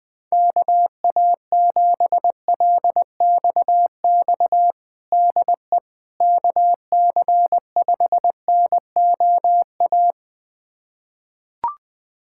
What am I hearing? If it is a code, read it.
Morse code, heard as a keyed tone, about 20 wpm, 700 Hz. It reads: KA7LXX DE KC5NOA
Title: USA Callsign DE Callsign 20wpm